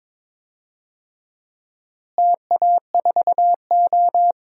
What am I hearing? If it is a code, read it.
Morse code, heard as a keyed tone, about 22 wpm, 700 Hz. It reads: TA4O
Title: DXCC-100 Call Signs - 4 Characters - Repeated 3x - 22wpm